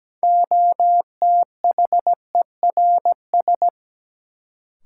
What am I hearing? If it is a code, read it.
Morse code, heard as a keyed tone, about 17 wpm, 700 Hz. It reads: OTHERS
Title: Sentences from Top 1000 Words - Review Part 2 - New Word Drill - 17wpm